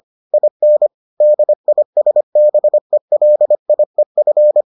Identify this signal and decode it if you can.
Morse code, heard as a keyed tone, about 25 wpm, 600 Hz. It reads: IN DISBELIEF